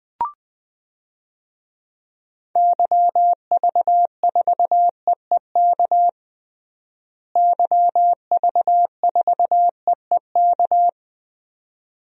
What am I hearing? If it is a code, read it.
Morse code, heard as a keyed tone, about 20 wpm, 700 Hz. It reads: YV4EEK YV4EEK
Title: DXCC-100 Call Signs - 6 Characters - Repeated 2x - 20wpm